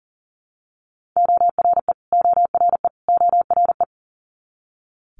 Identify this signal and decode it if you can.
Morse code, heard as a keyed tone, about 38 wpm, 700 Hz. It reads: ORE ORE ORE
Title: Multi-letter Phonemes Repeated 3x 40wpm